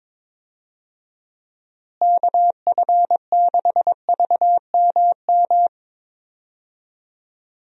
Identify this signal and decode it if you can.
Morse code, heard as a keyed tone, about 22 wpm, 700 Hz. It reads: KF6VMM
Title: USA Call Signs - 6 Characters - Repeated 3x - 22wpm